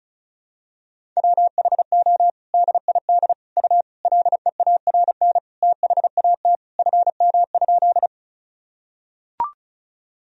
Morse code, heard as a keyed tone, about 36 wpm, 700 Hz. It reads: WHO DID U LEARN THAT FM?
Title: Sentences from Top 300 Words and Abbreviations 35wpm